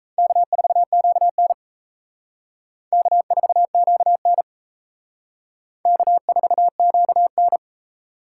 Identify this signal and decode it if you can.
Morse code, heard as a keyed tone, about 35 wpm, 700 Hz. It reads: K4QN K4QN K4QN
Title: Speed Racing - USA Call Signs 28wpm